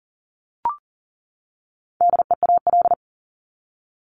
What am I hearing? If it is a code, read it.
Morse code, heard as a keyed tone, about 40 wpm, 700 Hz. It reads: DEAL
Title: Top 501-750 Words 40wpm